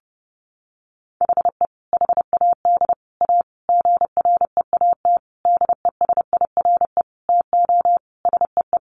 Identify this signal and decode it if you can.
Morse code, heard as a keyed tone, about 30 wpm, 700 Hz. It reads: HE HAD A GREAT DESIRE TO SEE
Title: Sentences from Top 1100 Words 30wpm